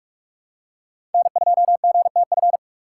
Morse code, heard as a keyed tone, about 47 wpm, 700 Hz. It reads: N2GTF
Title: USA Call Signs - 5 Characters - 6x Character Spacing - 45wpm